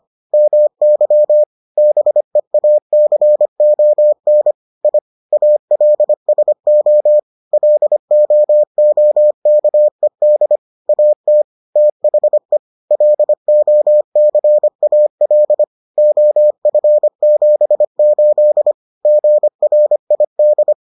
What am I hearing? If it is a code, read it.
Morse code, heard as a keyed tone, about 25 wpm, 600 Hz. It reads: MY BEACON I ALSO LOOKED AT THE LOCAL OF78 GRID